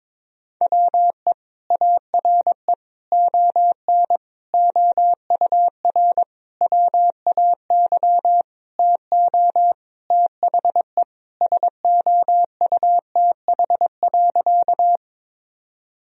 Morse code, heard as a keyed tone, about 22 wpm, 700 Hz. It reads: WE ARE ON OUR WAY TO THE SOUTH.
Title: Sentences from Top 1000 Words - Review Part 2 - 22wpm